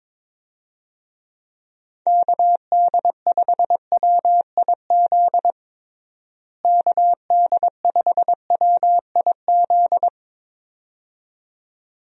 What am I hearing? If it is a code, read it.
Morse code, heard as a keyed tone, about 22 wpm, 700 Hz. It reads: KD5WIZ KD5WIZ
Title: USA Call Signs - 6 Characters - Repeated 2x - 22wpm